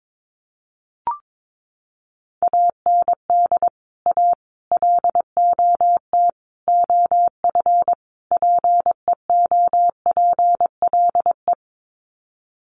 Morse code, heard as a keyed tone, about 22 wpm, 700 Hz. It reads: AND A LOT OF PEOPLE
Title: Sets of 5 Words 22wpm